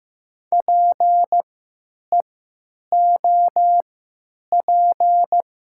Morse code, heard as a keyed tone, about 15 wpm, spaced as for 9 wpm, 700 Hz. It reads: PEOP
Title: 3x Character Spacing - Top 100 Words - 15wpm